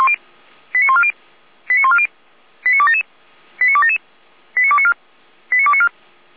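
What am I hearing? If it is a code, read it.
Five-tone selective call (ZVEI), heard as quick runs of five tones, tones of about 70 ms. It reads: 89140 89140 89277 89277 89294 89294
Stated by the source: ZVEI five-tone call 89187 received by SDR, channel 455.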